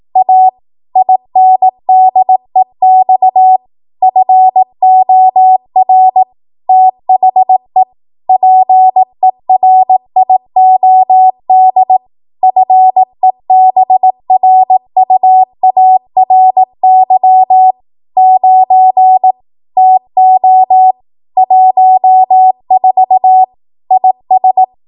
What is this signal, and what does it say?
Morse code, heard as a keyed tone, about 18 wpm, 750 Hz. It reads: A INDEX FOR THE PERIOD FEBRUARY 9 TO 14 IS
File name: Morse / 18WPM